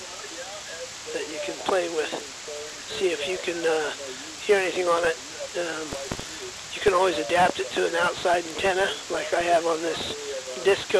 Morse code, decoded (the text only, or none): none